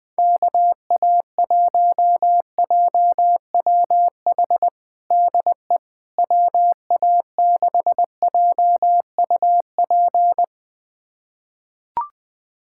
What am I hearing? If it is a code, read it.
Morse code, heard as a keyed tone, about 21 wpm, 700 Hz. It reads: KA1JWH DE WA6JUP
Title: USA Callsign DE Callsign - 6 Characters - 20wpm